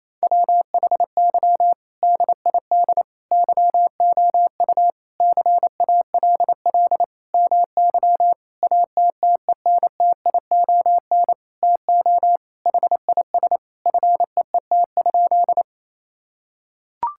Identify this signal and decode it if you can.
Morse code, heard as a keyed tone, about 28 wpm, 700 Hz. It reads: WHY DID YOU CALL MY ATTENTION TO HIS FEET?
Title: Sentences from Top 1000 Words - Review Part 6 - 28wpm